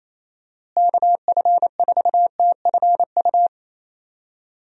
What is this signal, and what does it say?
Morse code, heard as a keyed tone, about 28 wpm, 700 Hz. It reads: KF4TFU
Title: USA Call Signs - 6 Characters - Repeated 3x - 28wpm